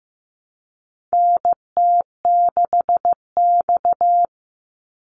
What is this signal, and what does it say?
Morse code, heard as a keyed tone, about 15 wpm, 700 Hz. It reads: NT6X
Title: Call Signs Repeated 2x - 15wpm